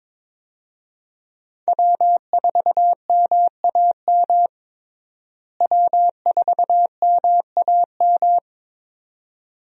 Morse code, heard as a keyed tone, about 22 wpm, 700 Hz. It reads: W4MAM W4MAM
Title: USA Call Signs - 5 Characters - Repeated 2x - 22wpm